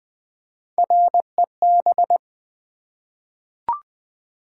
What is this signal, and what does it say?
Morse code, heard as a keyed tone, about 20 wpm, 700 Hz. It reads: REB
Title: Three Letter-Number Combinations - 20wpm